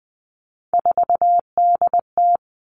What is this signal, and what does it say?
Morse code, heard as a keyed tone, about 19 wpm, 700 Hz. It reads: 4DT
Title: All Cut Numbers - 3-Digit 20wpm